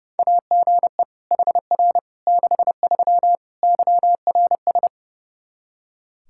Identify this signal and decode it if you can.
Morse code, heard as a keyed tone, about 32 wpm, 700 Hz. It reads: AGE HR 63 YRS
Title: Practice QSO - Exchange 3 - More Info - 30wpm